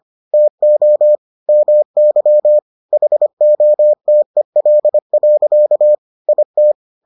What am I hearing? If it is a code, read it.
Morse code, heard as a keyed tone, about 25 wpm, 600 Hz. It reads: TO MY HOTEL. IT